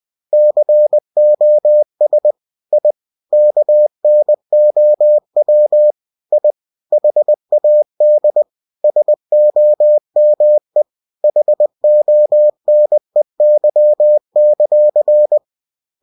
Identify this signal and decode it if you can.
Morse code, heard as a keyed tone, about 20 wpm, 600 Hz. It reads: COS I KNOW I HAD SOME HONEY;
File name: cw-books / winnie-the-pooh / winnie-20wpm-page-0039